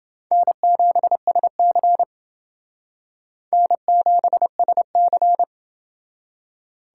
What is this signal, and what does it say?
Morse code, heard as a keyed tone, about 27 wpm, 700 Hz. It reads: N7SC N7SC